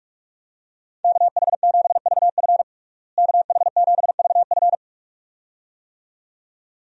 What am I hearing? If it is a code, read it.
Morse code, heard as a keyed tone, about 42 wpm, 700 Hz. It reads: KH7VF KH7VF
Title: USA Call Signs - 5 Characters - Repeated 2x - 45wpm